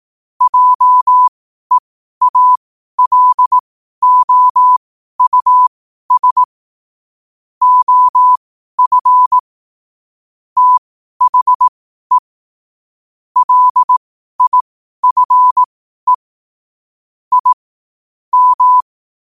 Morse code, heard as a keyed tone, about 18 wpm, spaced as for 12 wpm, 1000 Hz. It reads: JEALOUS OF THE LIFE I M
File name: qotdfiles / 12wpm